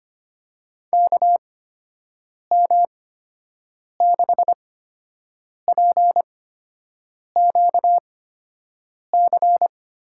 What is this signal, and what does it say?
Morse code, heard as a keyed tone, about 25 wpm, spaced as for 7 wpm, 700 Hz. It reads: KM6PQC
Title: USA Call Signs - 6 Characters - 8x Character Spacing - 25wpm